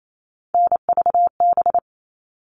Morse code, heard as a keyed tone, about 28 wpm, 700 Hz. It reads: NVB
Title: All Cut Numbers - 3-Digit 28wpm